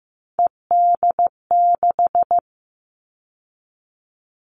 Morse code, heard as a keyed tone, about 15 wpm, 700 Hz. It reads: ED6
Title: Callsign Prefixes for Top 50 DXCC - 15wpm